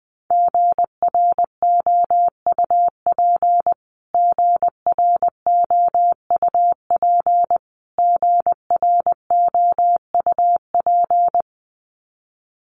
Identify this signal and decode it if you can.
Morse code, heard as a keyed tone, about 20 wpm, 700 Hz. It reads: GROUP GROUP GROUP
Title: Top 300 Words Repeated 20wpm